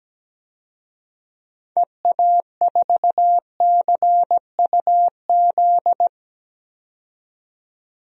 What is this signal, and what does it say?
Morse code, heard as a keyed tone, about 17 wpm, 700 Hz. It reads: EA4CUZ
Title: DXCC-100 Call Signs - 6 Characters - 17wpm